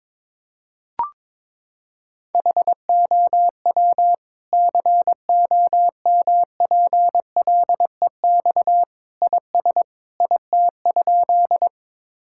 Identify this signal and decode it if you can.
Morse code, heard as a keyed tone, about 22 wpm, 700 Hz. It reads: HOW COMPLEX IS IT?
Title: Sentences from Top 1000 Words - Review Part 6 - 22wpm